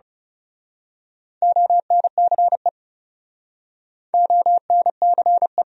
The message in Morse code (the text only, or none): ONCE ONCE